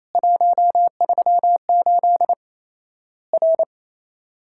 Morse code, heard as a keyed tone, about 28 wpm, 700 Hz. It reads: 138 R
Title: Worked All Europe DX Contest - QTC Exchange - 28wpm